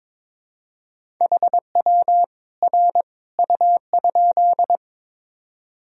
Morse code, heard as a keyed tone, about 22 wpm, 700 Hz. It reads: HW R U?
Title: Sentences from Top 100 Words and Abbreviations 22wpm